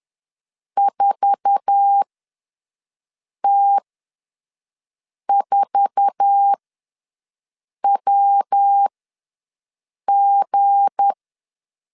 Morse code, heard as a keyed tone, about 10 wpm, spaced as for 5 wpm, 800 Hz. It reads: 4T4WG
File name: LEZIONI / 8 YJ45 / 8 YJ4 ws 10wpm 4020